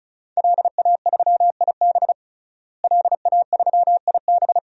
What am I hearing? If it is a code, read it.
Morse code, heard as a keyed tone, about 36 wpm, 700 Hz. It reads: LA3IB LA3IB
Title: DXCC-100 Call Signs - 5 Characters - Repeated 3x - 35wpm